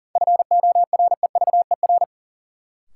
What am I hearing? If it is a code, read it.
Morse code, heard as a keyed tone, about 40 wpm, 700 Hz. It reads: FOREVER
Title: Sentences from Top 1600 Words - New Word Drill - 40wpm